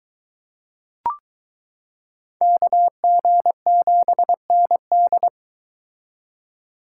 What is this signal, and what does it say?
Morse code, heard as a keyed tone, about 23 wpm, 700 Hz. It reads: KG7ND